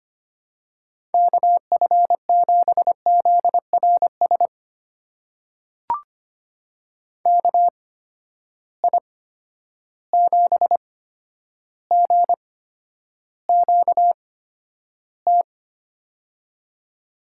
Morse code, heard as a keyed tone, about 25 wpm, spaced as for 11 wpm, 700 Hz. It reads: KF7ZRS KI7GQT
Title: USA Call Signs - 6 Characters - 8x Character Spacing - 25wpm